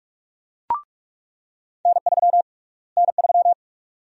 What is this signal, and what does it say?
Morse code, heard as a keyed tone, about 42 wpm, 700 Hz. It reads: N3 N3
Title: USA Call Sign Prefixes - Repeated 2x - 45wpm